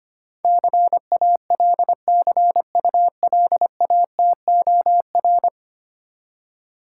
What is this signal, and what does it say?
Morse code, heard as a keyed tone, about 25 wpm, 690 Hz. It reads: CALCULATOR
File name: practice / 25wpm / Easy Words with Hints - 25wpm